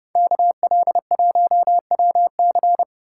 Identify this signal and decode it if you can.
Morse code, heard as a keyed tone, about 35 wpm, 700 Hz. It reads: KL1WC
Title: Speed Racing - USA 2x2 Call Signs 30wpm